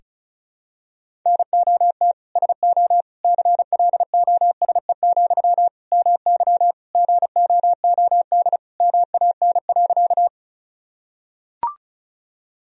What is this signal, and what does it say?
Morse code, heard as a keyed tone, about 36 wpm, 700 Hz. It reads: NOT SO CLOSE, MY GOOD MAN.
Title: Sentences from Top 1000 Words - Review Part 1 - 35wpm